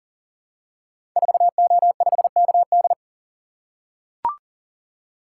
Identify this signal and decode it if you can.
Morse code, heard as a keyed tone, about 38 wpm, 700 Hz. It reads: 4O5KD
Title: Code Groups - 5 Letter:Numbers - 40wpm